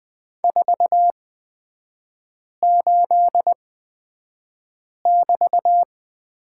Morse code, heard as a keyed tone, about 20 wpm, spaced as for 5 wpm, 700 Hz. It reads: 48<BT>
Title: 224 Lesson 36e - BT - Three Characters 20wpm